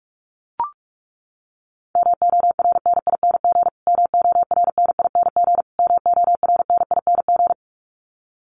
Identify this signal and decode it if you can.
Morse code, heard as a keyed tone, about 44 wpm, 700 Hz. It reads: MORNING MORNING MORNING
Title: Top 500 Words Repeated 45wpm